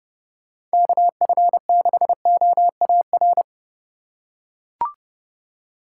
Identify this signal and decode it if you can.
Morse code, heard as a keyed tone, about 30 wpm, 700 Hz. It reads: KF6OAR